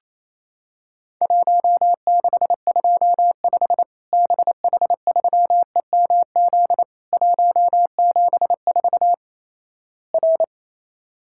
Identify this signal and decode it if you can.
Morse code, heard as a keyed tone, about 28 wpm, 700 Hz. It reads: 1625 BH3EMZ 174 R
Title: Worked All Europe DX Contest - QTC Exchange - 28wpm